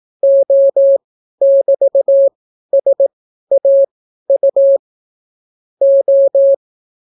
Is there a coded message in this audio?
Yes: Morse code from 0.2 to 6.5 s, heard as a keyed tone, about 18 wpm, spaced as for 12 wpm, 550 Hz. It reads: O<BT>SAU O